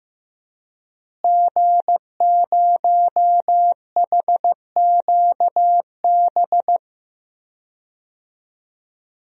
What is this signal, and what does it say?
Morse code, heard as a keyed tone, about 14 wpm, 700 Hz. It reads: G0HQB
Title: DXCC-100 Call Signs - 5 Characters - Repeated 3x - 15wpm